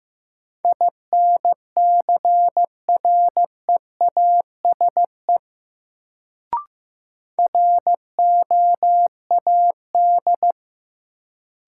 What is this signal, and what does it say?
Morse code, heard as a keyed tone, about 15 wpm, 700 Hz. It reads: INCREASE ROAD
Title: Sets of 5 Words - New Word Drill - 15wpm